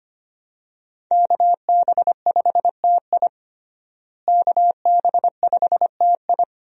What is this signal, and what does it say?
Morse code, heard as a keyed tone, about 25 wpm, 700 Hz. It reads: KB5TI KB5TI
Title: USA Call Signs - 5 Characters - Repeated 2x - 25wpm